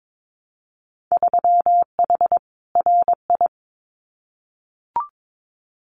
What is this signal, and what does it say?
Morse code, heard as a keyed tone, about 22 wpm, 700 Hz. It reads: 3H RI
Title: Winter Field Day Exchanges Encore 22wpm